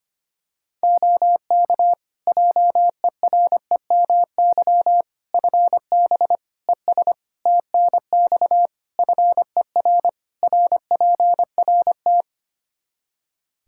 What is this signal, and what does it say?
Morse code, heard as a keyed tone, about 25 wpm, 700 Hz. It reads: OK JEREMY FB ES TNX FER RPRT
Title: Practice QSO - Exchange 2 - Info - 25wpm